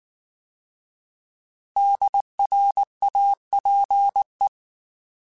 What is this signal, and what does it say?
Morse code, heard as a keyed tone, about 19 wpm, 800 Hz. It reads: DRAPE